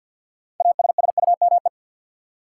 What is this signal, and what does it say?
Morse code, heard as a keyed tone, about 53 wpm, 700 Hz. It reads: ASSUME
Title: Sentences from Top 1500 Words - New Word Drill Alt - 50wpm